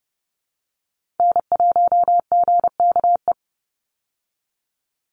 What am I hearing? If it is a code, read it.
Morse code, heard as a keyed tone, about 30 wpm, 700 Hz. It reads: N1GKE
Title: Call Signs Repeated 3x - 30wpm